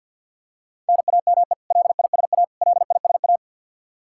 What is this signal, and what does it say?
Morse code, heard as a keyed tone, about 53 wpm, 700 Hz. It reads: NAME LISA LISA
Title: Practice QSO - Exchange 1 - Beginning - 50wpm